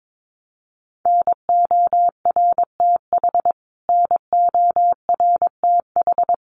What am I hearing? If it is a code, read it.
Morse code, heard as a keyed tone, about 22 wpm, 700 Hz. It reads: NORTH NORTH
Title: Top 300 Words Repeated 22wpm